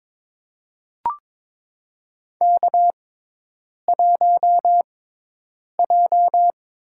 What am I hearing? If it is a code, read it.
Morse code, heard as a keyed tone, about 22 wpm, spaced as for 8 wpm, 700 Hz. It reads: K1J
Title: USA Call Signs - 6x Character Spacing - 22wpm